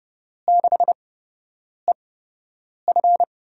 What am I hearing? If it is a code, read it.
Morse code, heard as a keyed tone, about 30 wpm, spaced as for 8 wpm, 700 Hz. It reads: 6EF
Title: USA Call Signs - 8x Character Spacing - 30wpm